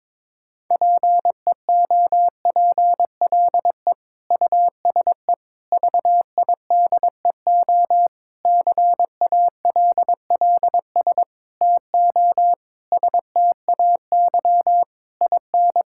Morse code, heard as a keyed tone, about 22 wpm, 700 Hz. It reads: PEOPLE USE VIDEO CALLS TO STAY IN
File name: Difficult Sentences with Hints - 22wpm